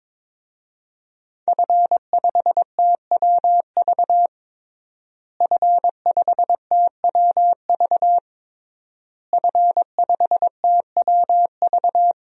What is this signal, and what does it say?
Morse code, heard as a keyed tone, about 23 wpm, 700 Hz. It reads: F5TWV F5TWV F5TWV
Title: DXCC-100 Call Signs - 5 Characters - Repeated 3x - 22wpm